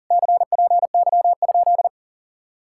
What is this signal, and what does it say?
Morse code, heard as a keyed tone, about 47 wpm, 700 Hz. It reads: CPY?
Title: Speed-Racing Common QSO Phrases Encore - 40wpm